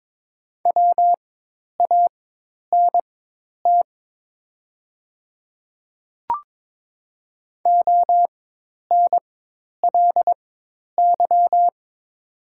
Morse code, heard as a keyed tone, about 22 wpm, spaced as for 10 wpm, 700 Hz. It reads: WANT ONLY
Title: 4x Character Spacing - Top 200 Words - 22wpm